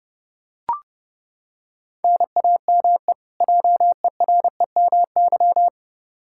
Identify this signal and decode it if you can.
Morse code, heard as a keyed tone, about 30 wpm, 700 Hz. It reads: NAME JEREMY
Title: K1USN SST QSO Practice 30wpm